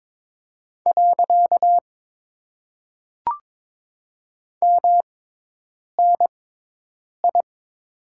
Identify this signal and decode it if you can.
Morse code, heard as a keyed tone, about 22 wpm, spaced as for 8 wpm, 700 Hz. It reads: . MNI
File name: Top 90 QSO Elements - 6x Character Spacing - 22wpm